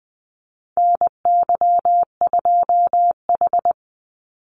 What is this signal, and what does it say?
Morse code, heard as a keyed tone, about 20 wpm, 700 Hz. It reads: NY2H
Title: Call Signs Repeated 2x - 20wpm